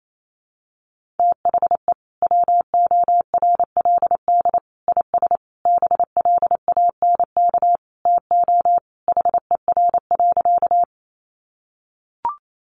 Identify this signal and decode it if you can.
Morse code, heard as a keyed tone, about 28 wpm, 700 Hz. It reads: THE WORLD IS BLANK TO HER.
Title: Sentences from Top 1500 Words 28wpm